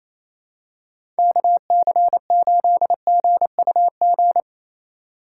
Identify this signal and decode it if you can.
Morse code, heard as a keyed tone, about 28 wpm, 700 Hz. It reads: KC8GUG
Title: USA Call Signs - 3x Character Spacing - 28wpm